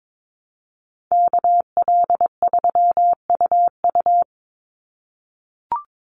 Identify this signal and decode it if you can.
Morse code, heard as a keyed tone, about 22 wpm, 700 Hz. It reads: KL3UU
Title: USA Call Signs 22wpm